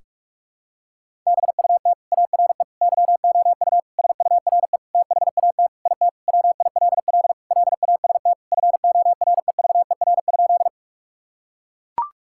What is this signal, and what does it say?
Morse code, heard as a keyed tone, about 41 wpm, 700 Hz. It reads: BUT ARE YOU SURE THAT IT WILL LAST FOREVER?
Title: Sentences from Top 1600 Words 45wpm